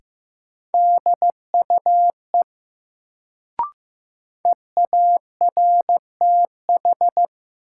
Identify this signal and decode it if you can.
Morse code, heard as a keyed tone, about 15 wpm, 700 Hz. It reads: DUE EARTH
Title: Sets of 5 Words Encore - New Word Drill - 15wpm